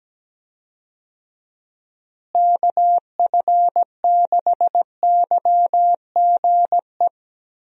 Morse code, heard as a keyed tone, about 17 wpm, 700 Hz. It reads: KF6YGE
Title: USA Call Signs - 6 Characters - Repeated 2x - 17wpm